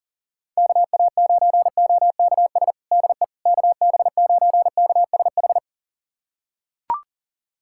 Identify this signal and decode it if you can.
Morse code, heard as a keyed tone, about 40 wpm, 700 Hz. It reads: KA9OKS DE KB9KSH
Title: USA Callsign DE Callsign - 6 Characters - 40wpm